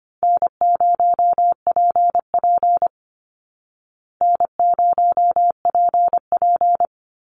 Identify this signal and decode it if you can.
Morse code, heard as a keyed tone, about 24 wpm, 700 Hz. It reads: N0PP N0PP
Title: Easy Call Signs Repeated 2x - 25wpm